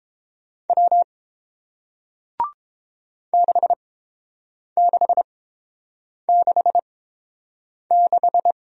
Morse code, heard as a keyed tone, about 30 wpm, spaced as for 7 wpm, 700 Hz. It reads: W 6666